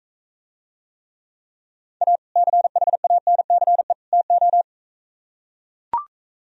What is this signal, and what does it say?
Morse code, heard as a keyed tone, about 45 wpm, 700 Hz. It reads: A CHANCE TO